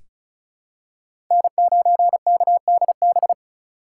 Morse code, heard as a keyed tone, about 34 wpm, 700 Hz. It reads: N9KDB
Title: K1USN SST QSO Practice - 35wpm